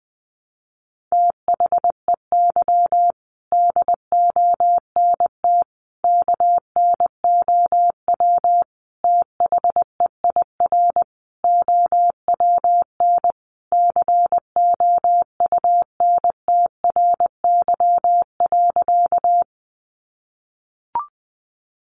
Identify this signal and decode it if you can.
Morse code, heard as a keyed tone, about 19 wpm, 700 Hz. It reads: THEY DONT KNOW THEIR OWN COUNTRY.
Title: Sentences from Top 300 Words 20wpm